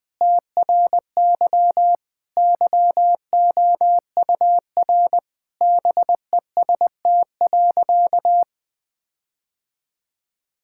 Morse code, heard as a keyed tone, about 20 wpm, 700 Hz. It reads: TRY YOUR BEST.
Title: Sentences with 7-Word Phrases - 20wpm